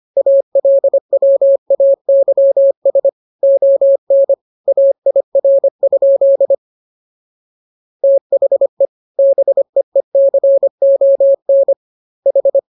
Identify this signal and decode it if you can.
Morse code, heard as a keyed tone, about 25 wpm, 550 Hz. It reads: ALWAYS ON AIR? THE BEECON H